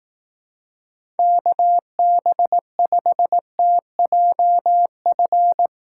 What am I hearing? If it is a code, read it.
Morse code, heard as a keyed tone, about 18 wpm, 700 Hz. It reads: KB5TJF
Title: Speed Racing - USA 2x3 Call Signs 15wpm